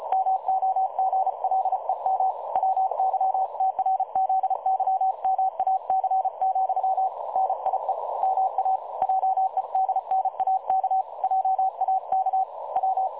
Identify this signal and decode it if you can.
Morse code, heard as a keyed tone, about 33 wpm, 760 Hz. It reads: MOGLEM WYTRZYMAC, NIE MA JEDNAK JAK W
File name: dlazaawansowanych / nagraniazpasma / websdr recording 3565.0kHz